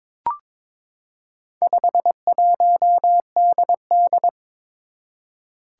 Morse code, heard as a keyed tone, about 22 wpm, 700 Hz. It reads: 51DD
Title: Four Challenging Letter-Number Combinations 22wpm